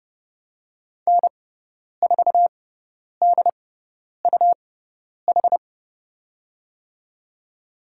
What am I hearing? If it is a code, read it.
Morse code, heard as a keyed tone, about 31 wpm, spaced as for 10 wpm, 700 Hz. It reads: N4DUH
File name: Lesson 17g - Letter u - Call Signs 30wpm